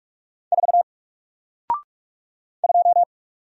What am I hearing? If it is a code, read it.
Morse code, heard as a keyed tone, about 44 wpm, 700 Hz. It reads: 4 2
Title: Single Character (Letter:Number:Punctuation:Prosign) Alt 45wpm